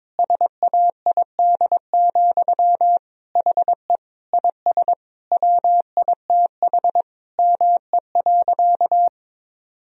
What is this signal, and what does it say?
Morse code, heard as a keyed tone, about 22 wpm, 700 Hz. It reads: SAID, HE IS WITH ME.
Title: Sentences from Top 1000 Words - Review Part 3 - 22wpm